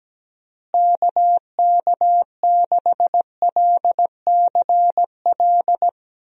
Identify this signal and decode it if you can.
Morse code, heard as a keyed tone, about 17 wpm, 700 Hz. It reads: KK6LCL
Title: USA Call Signs - 6 Characters - Repeated 2x - 17wpm